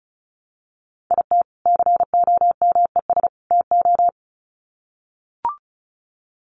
Morse code, heard as a keyed tone, about 34 wpm, 700 Hz. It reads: IT COMES TO
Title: Sets of 3 Words 35wpm